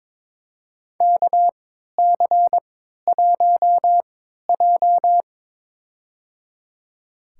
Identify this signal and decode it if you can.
Morse code, heard as a keyed tone, about 21 wpm, spaced as for 12 wpm, 700 Hz. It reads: KC1J
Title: USA Call Signs - 4 Characters - 3x Character Spacing - 22wpm